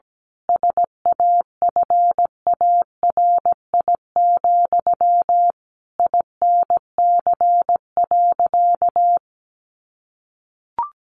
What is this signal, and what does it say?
Morse code, heard as a keyed tone, about 17 wpm, 700 Hz. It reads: SAFARI, INC.
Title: A Sound Of Thunder 17wpm